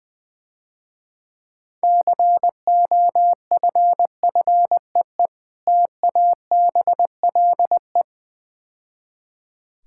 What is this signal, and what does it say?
Morse code, heard as a keyed tone, about 20 wpm, 700 Hz. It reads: COFFEE TABLE
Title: Easy Words - Speed-Racing with Hints - 20wpm